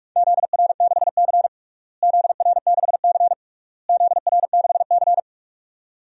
Morse code, heard as a keyed tone, about 44 wpm, 700 Hz. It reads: ZR6C ZR6C ZR6C
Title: DXCC-100 Call Signs - 4 Characters - Repeated 3x - 45wpm